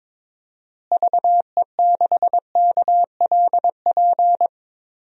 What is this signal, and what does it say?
Morse code, heard as a keyed tone, about 22 wpm, 700 Hz. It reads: VE6KLP
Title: DXCC-100 Call Signs - 6 Characters - Repeated 2x - 22wpm